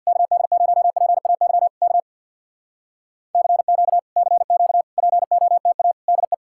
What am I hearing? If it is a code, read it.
Morse code, heard as a keyed tone, about 64 wpm, 700 Hz. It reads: KD0PAY K CQ CQ POTA DE